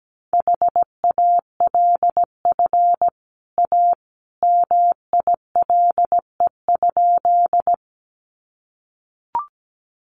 Morse code, heard as a keyed tone, about 17 wpm, 700 Hz. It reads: HALF A MILE?
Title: Sentences from Top 400 Words 17wpm